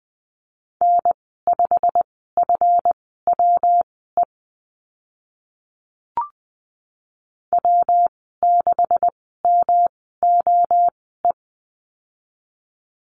Morse code, heard as a keyed tone, about 20 wpm, spaced as for 14 wpm, 700 Hz. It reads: N5FWE W6MOE
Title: 147 Lesson 24g - Letter y - Call Signs 20wpm